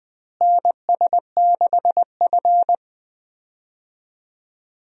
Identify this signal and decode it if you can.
Morse code, heard as a keyed tone, about 20 wpm, 700 Hz. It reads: NS6F
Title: USA Call Signs - 4 Characters - Repeated 3x - 20wpm